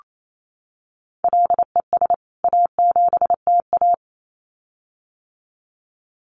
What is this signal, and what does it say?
Morse code, heard as a keyed tone, about 28 wpm, 700 Hz. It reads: LES A7TA
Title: CWT Exchange 28wpm